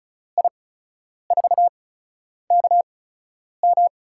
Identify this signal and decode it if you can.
Morse code, heard as a keyed tone, about 36 wpm, spaced as for 10 wpm, 700 Hz. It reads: I4KM
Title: USA Call Signs - 5 Characters - 8x Character Spacing - 35wpm